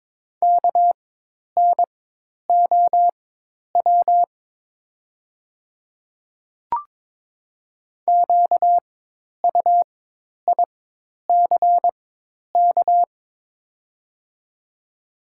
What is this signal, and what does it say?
Morse code, heard as a keyed tone, about 22 wpm, spaced as for 10 wpm, 690 Hz. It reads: KNOW QUICK
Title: 4x Character Spacing - Top 500 Words - 22wpm